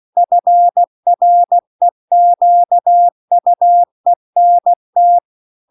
Morse code, heard as a keyed tone, about 16 wpm, 700 Hz. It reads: FREQUENT